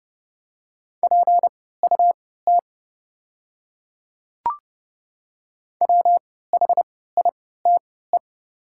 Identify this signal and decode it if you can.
Morse code, heard as a keyed tone, about 30 wpm, spaced as for 17 wpm, 690 Hz. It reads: PUT WHITE
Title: 3x Character Spacing - Top 500 Words - 30wpm